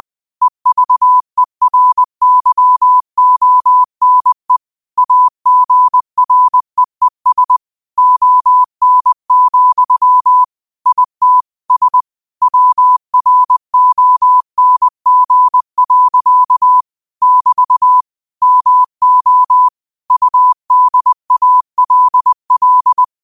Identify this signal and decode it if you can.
Morse code, heard as a keyed tone, about 20 wpm, 1000 Hz. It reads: EVERYONE AGREES ON, IT S WRONG. <BT> MO UDALL
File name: qotdfiles / 20wpm